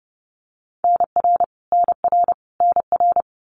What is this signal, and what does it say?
Morse code, heard as a keyed tone, about 30 wpm, 700 Hz. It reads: NR NR NR
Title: Abbreviations Repeated 3x - 30wpm